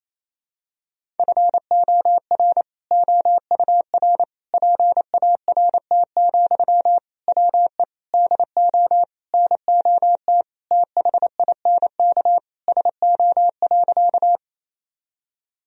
Morse code, heard as a keyed tone, about 28 wpm, 700 Hz. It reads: FOR OUR PART, WE DO NOT THINK SO.
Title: Sentences from Top 1000 Words - Review Part 1 - 28wpm